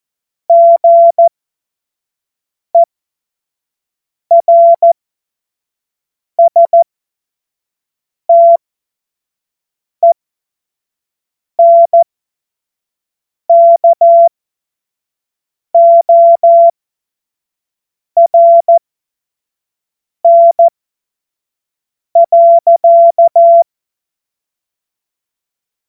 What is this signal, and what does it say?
Morse code, heard as a keyed tone, about 14 wpm, spaced as for 5 wpm, 680 Hz. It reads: GERSTENKORN.
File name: Morsekurs / Witze / Witz 23 April 2024 25 F70